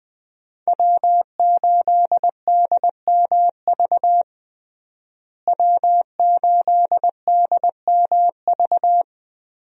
Morse code, heard as a keyed tone, about 20 wpm, 700 Hz. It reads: W8DMV W8DMV
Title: USA Call Signs - 5 Characters - Repeated 2x - 20wpm